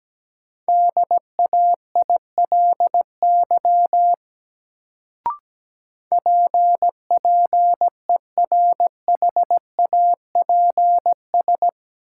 Morse code, heard as a keyed tone, about 17 wpm, 700 Hz. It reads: DAILY PPERHAPS
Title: Sentences from Top 1100 Words - New Word Drill Alt - 17wpm